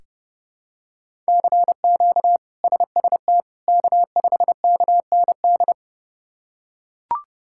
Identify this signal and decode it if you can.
Morse code, heard as a keyed tone, about 30 wpm, 700 Hz. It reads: CQ SST K5KND
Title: K1USN SST QSO Practice - 30wpm